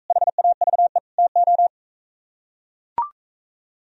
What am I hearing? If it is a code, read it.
Morse code, heard as a keyed tone, about 45 wpm, 700 Hz. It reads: HAVE TO